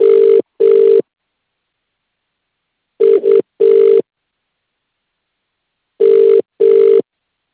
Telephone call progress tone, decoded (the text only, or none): ringback tone